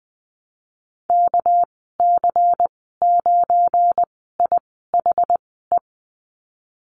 Morse code, heard as a keyed tone, about 20 wpm, spaced as for 14 wpm, 700 Hz. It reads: KC9IHE
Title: 213 Lesson 34g - Number 8 - Call Signs 20wpm